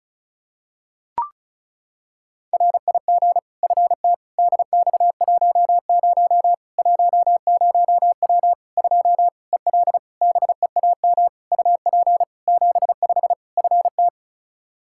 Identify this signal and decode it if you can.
Morse code, heard as a keyed tone, about 36 wpm, 700 Hz. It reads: RIG FT DX10 10W 2 EL BEAM UP 75 FT
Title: K1USN SST QSO Practice 35wpm